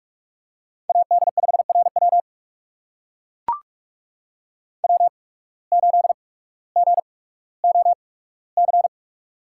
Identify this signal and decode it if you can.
Morse code, heard as a keyed tone, about 42 wpm, spaced as for 19 wpm, 700 Hz. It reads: AD5RW W8GOC
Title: USA Call Signs - 5 Characters - 8x Character Spacing - 45wpm